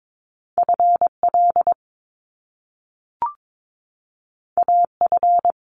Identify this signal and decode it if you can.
Morse code, heard as a keyed tone, about 22 wpm, 700 Hz. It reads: FL AF
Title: Top Two Letter Combinations in Words 22wpm